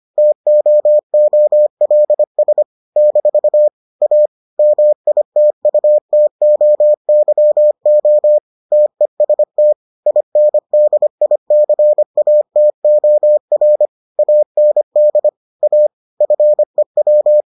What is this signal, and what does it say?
Morse code, heard as a keyed tone, about 25 wpm, 600 Hz. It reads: TOOLS - A MITUTOYO TEST INDICATOR AND A FEW